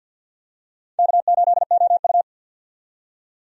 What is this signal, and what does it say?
Morse code, heard as a keyed tone, about 46 wpm, 700 Hz. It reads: K8OU
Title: USA Call Signs - 4 Characters - 50wpm